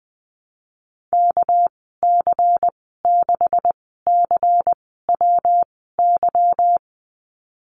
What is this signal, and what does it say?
Morse code, heard as a keyed tone, about 20 wpm, spaced as for 15 wpm, 700 Hz. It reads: KC6CWY
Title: 213 Lesson 34g - Number 8 - Call Signs 20wpm